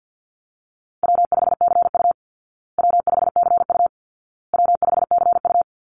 Morse code, heard as a keyed tone, about 50 wpm, 700 Hz. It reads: W5CU W5CU W5CU
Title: Call Signs Repeated 2x - 50wpm